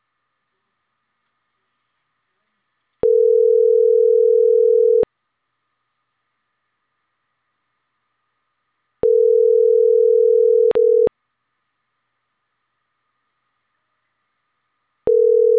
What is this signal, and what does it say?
Ringback tone (North America), 440 + 480 Hz.